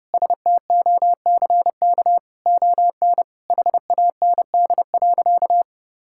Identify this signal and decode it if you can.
Morse code, heard as a keyed tone, about 30 wpm, 700 Hz. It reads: STOCK ON HAND.
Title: Sentences from Top 1000 Words - Review Part 5 - 30wpm